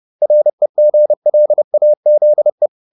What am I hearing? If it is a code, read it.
Morse code, heard as a keyed tone, about 30 wpm, 600 Hz. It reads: REGLAZE